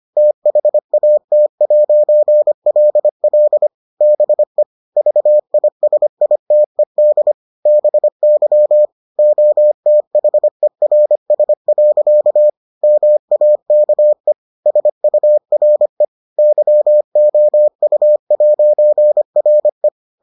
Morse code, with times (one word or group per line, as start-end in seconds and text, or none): 0.2-3.7 THAT'LL
4.0-4.6 BE
5.0-7.3 VISITED
7.7-8.9 BY
9.2-12.5 OTHERS.
12.8-14.3 MAKE
14.7-16.0 SURE
16.4-19.9 YOU'RE